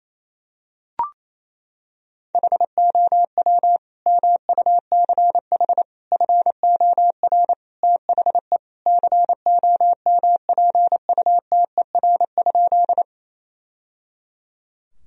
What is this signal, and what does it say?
Morse code, heard as a keyed tone, about 28 wpm, 700 Hz. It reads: HOW MUCH FOR THE COMPUTER?
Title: Sentences from Top 1000 Words - Review Part 3 - 28wpm